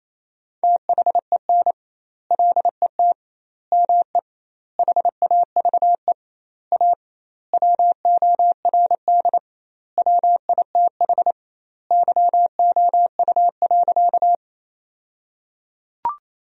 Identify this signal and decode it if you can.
Morse code, heard as a keyed tone, about 28 wpm, 700 Hz. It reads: THEN LET ME HAVE A WORD WITH YOU.
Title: 2x Word-Spacing - Sentences from Top 300 Words - 28wpm